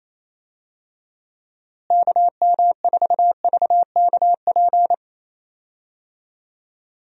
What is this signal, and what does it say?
Morse code, heard as a keyed tone, about 28 wpm, 700 Hz. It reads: KM4VKP
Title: USA Call Signs - 6 Characters - Repeated 2x - 28wpm